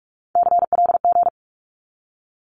Morse code, heard as a keyed tone, about 44 wpm, 700 Hz. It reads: CLG
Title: Abbreviations 45wpm